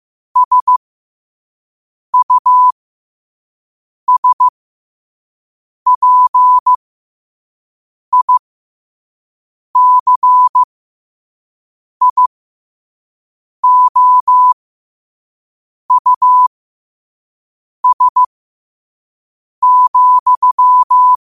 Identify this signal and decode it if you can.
Morse code, heard as a keyed tone, about 15 wpm, spaced as for 5 wpm, 1000 Hz. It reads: SUSPICIOUS,
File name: qotdfiles / 5wpm